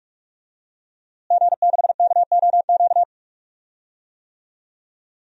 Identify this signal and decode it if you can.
Morse code, heard as a keyed tone, about 43 wpm, 700 Hz. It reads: G6KOQ
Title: DXCC-100 Call Signs - 5 Characters - Repeated 2x - 45wpm